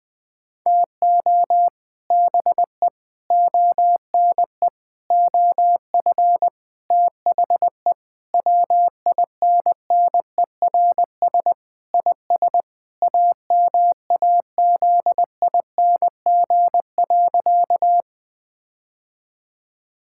Morse code, heard as a keyed tone, about 20 wpm, 700 Hz. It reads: TO BE ONE OF THE WINNERS IS AMAZING.
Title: Sentences with 5-Word Phrases - 20wpm